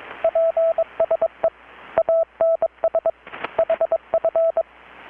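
Morse code, heard as a keyed tone, about 22 wpm, 660 Hz, with radio noise. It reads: PSE ANS HF
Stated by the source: Morse from KPH's Night of Nights XX maritime radiotelegraphy event, received in Chiapas, Mexico about 3,800 km away on a Kenwood R-600 with a grounded 10 m long-wire antenna.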